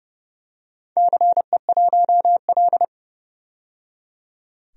Morse code, heard as a keyed tone, about 31 wpm, 700 Hz. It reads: CE1L
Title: DXCC-100 Call Signs - 4 Characters - 30wpm